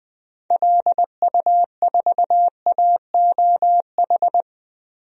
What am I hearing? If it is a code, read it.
Morse code, heard as a keyed tone, about 21 wpm, 700 Hz. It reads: LU4AOH
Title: DXCC-100 Call Signs - 6 Characters - 20wpm